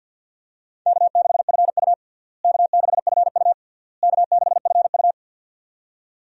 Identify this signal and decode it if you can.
Morse code, heard as a keyed tone, about 52 wpm, 700 Hz. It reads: K6FU K6FU K6FU
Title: USA Call Signs - 4 Characters - Repeated 3x - 50wpm